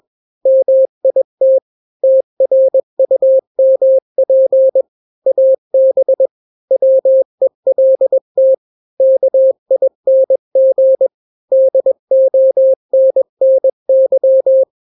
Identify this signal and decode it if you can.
Morse code, heard as a keyed tone, about 21 wpm, 530 Hz. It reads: MIT TRUMP AB WELT KING DONNY
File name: daily / 20wpm